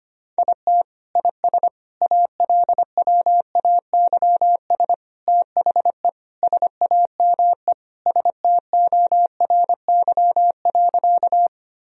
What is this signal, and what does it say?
Morse code, heard as a keyed tone, about 25 wpm, 700 Hz. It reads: IT IS ALWAYS THE SAME STORY.
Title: Sentences from Top 1000 Words - Review Part 1 - 25wpm